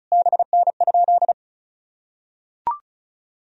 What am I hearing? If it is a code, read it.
Morse code, heard as a keyed tone, about 34 wpm, 700 Hz. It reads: BN?
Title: Sentences from Top 400 Words and Abbreviations 35wpm